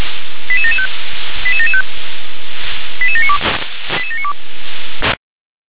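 Five-tone selective call (ZVEI), heard as quick runs of five tones, tones of about 70 ms. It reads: 88775 88775 88792 88792
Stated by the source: ZVEI five-tone selective call received by SDR.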